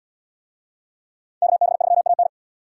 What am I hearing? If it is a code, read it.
Morse code, heard as a keyed tone, about 77 wpm, 700 Hz. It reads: KB3AN